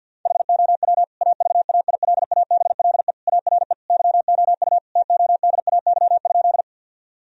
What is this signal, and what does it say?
Morse code, heard as a keyed tone, about 52 wpm, 700 Hz. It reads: HOW AVAILABLE ARE YOU TODAY?
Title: Sentences from Top 1000 Words - Review Part 3 - 50wpm